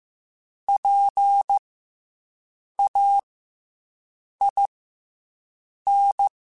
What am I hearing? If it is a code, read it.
Morse code, heard as a keyed tone, about 15 wpm, 800 Hz. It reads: P A I N
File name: Special INGs / Special INGs-mp3s / SlowToFaster ing @13